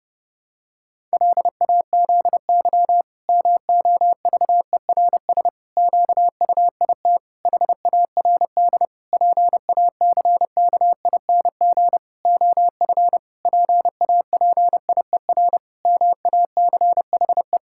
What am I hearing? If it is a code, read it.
Morse code, heard as a keyed tone, about 30 wpm, 700 Hz. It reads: LAZY MOVERS QUIT HARD PACKING OF PAPIER MACHE
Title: Pangram Challenge v2 - 30wpm